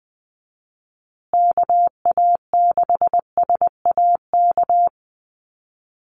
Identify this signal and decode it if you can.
Morse code, heard as a keyed tone, about 19 wpm, 700 Hz. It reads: KA6SAK
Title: USA Call Signs Repeated 2x 20wpm